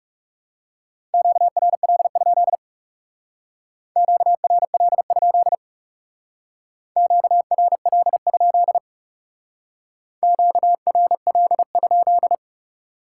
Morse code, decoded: QRL? QRL? QRL? QRL?

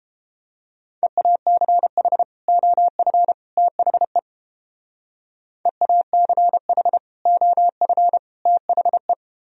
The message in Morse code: EACH OF THE EACH OF THE

